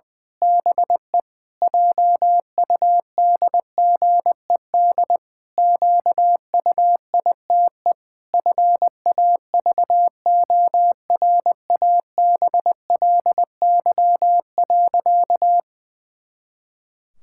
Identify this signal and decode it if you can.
Morse code, heard as a keyed tone, about 20 wpm, 700 Hz. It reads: BE JUDGED QUITE FAVORABLY.
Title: Pangram Challenge v2 - 20wpm